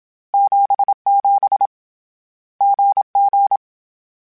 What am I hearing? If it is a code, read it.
Morse code, heard as a keyed tone, about 27 wpm, 800 Hz. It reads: 77 GG